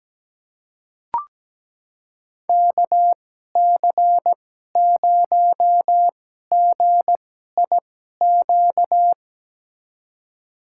Morse code, heard as a keyed tone, about 17 wpm, spaced as for 12 wpm, 700 Hz. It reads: KC0GIQ
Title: USA Call Signs - 6 Characters - 2x Character Spacing - 17wpm